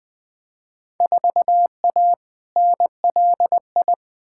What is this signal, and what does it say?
Morse code, heard as a keyed tone, about 20 wpm, 700 Hz. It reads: 4A NLI
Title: Field Day Exchanges Encore - 20wpm